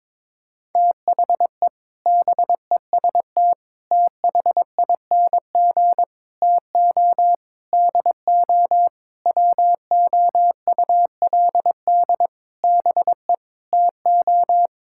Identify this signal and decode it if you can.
Morse code, heard as a keyed tone, about 22 wpm, 700 Hz. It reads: THE BEST THING TO DO WOULD BE TO
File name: practice / 22wpm / Sentences with 8-Word Phrases - 22wpm